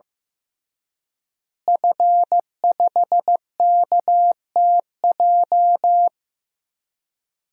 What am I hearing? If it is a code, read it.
Morse code, heard as a keyed tone, about 15 wpm, 700 Hz. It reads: F5KTJ
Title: DXCC-100 Call Signs - 5 Characters - Repeated 3x - 15wpm